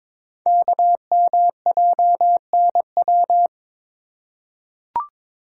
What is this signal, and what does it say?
Morse code, heard as a keyed tone, about 22 wpm, 700 Hz. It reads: KMJNW
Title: Code Groups - 5 Letters - 22wpm